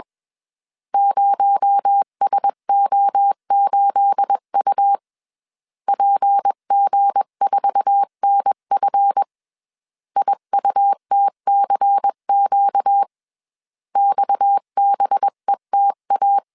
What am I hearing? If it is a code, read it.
Morse code, heard as a keyed tone, about 21 wpm, 800 Hz. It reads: 0SO8U PG4NF IUTCQ XBETA
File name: CW / DET / 1 9 1500C S B 20wpm 2313